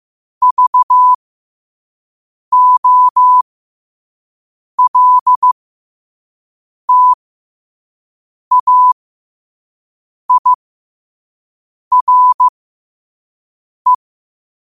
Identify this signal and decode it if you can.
Morse code, heard as a keyed tone, about 15 wpm, spaced as for 5 wpm, 1000 Hz. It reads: VOLTAIRE
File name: qotdfiles / 5wpm